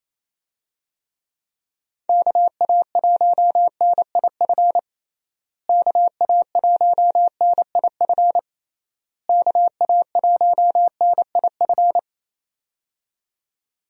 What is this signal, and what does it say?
Morse code, heard as a keyed tone, about 28 wpm, 700 Hz. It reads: KA1NIF KA1NIF KA1NIF
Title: USA Call Signs - 6 Characters - Repeated 3x - 28wpm